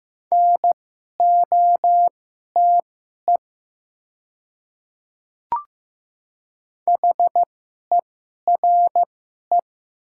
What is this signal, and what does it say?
Morse code, heard as a keyed tone, about 15 wpm, spaced as for 11 wpm, 700 Hz. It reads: NOTE HERE
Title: 2x Character Spacing - Top 500 Words - 15wpm